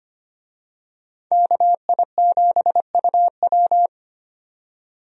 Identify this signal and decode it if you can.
Morse code, heard as a keyed tone, about 25 wpm, 700 Hz. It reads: KI7UW
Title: USA Call Signs - 5 Characters - 25wpm